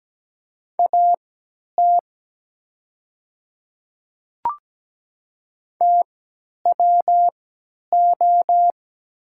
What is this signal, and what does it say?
Morse code, heard as a keyed tone, about 17 wpm, spaced as for 10 wpm, 700 Hz. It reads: AT TWO
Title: 3x Character Spacing - Top 200 Words - 17wpm